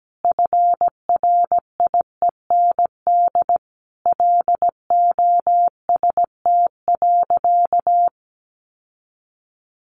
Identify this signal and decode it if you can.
Morse code, heard as a keyed tone, about 17 wpm, 700 Hz. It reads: FRIEND LOST.
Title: Sentences from Top 1400 Words 17wpm